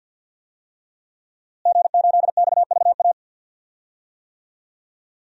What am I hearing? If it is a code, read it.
Morse code, heard as a keyed tone, about 52 wpm, 700 Hz. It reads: G8XVA
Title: DXCC-100 Call Signs - 5 Characters - Repeated 2x - 50wpm